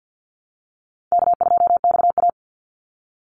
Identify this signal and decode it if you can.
Morse code, heard as a keyed tone, about 50 wpm, 700 Hz. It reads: K2XA
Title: Easy Call Signs Repeated 2x - 50wpm